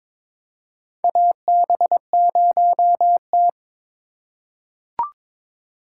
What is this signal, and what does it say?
Morse code, heard as a keyed tone, about 22 wpm, 700 Hz. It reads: AB0T
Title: USA Call Signs - 4x Character Spacing - 22wpm